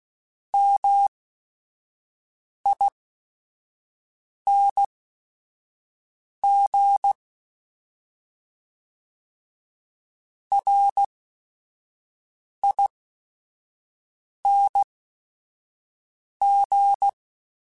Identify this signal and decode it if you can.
Morse code, heard as a keyed tone, about 16 wpm, spaced as for 5 wpm, 800 Hz. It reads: MING RING